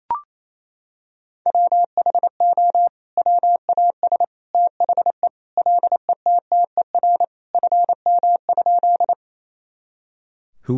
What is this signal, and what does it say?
Morse code, heard as a keyed tone, about 28 wpm, 700 Hz. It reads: WHO WAS THE LETTER FM?
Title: Sentences from Top 300 Words and Abbreviations 28wpm